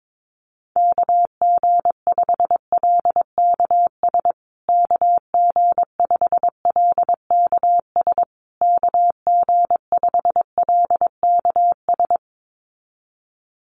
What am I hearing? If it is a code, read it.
Morse code, heard as a keyed tone, about 22 wpm, 700 Hz. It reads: KG5LKS KG5LKS KG5LKS
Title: USA Call Signs Repeated 3x 22wpm